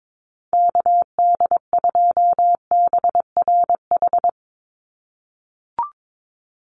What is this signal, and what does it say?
Morse code, heard as a keyed tone, about 22 wpm, 700 Hz. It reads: KD2BRH
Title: USA Call Signs 22wpm